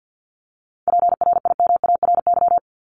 Morse code, heard as a keyed tone, about 50 wpm, 700 Hz. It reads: PRIMARY
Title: LY Suffix - Top 100 Words with Base Words 50wpm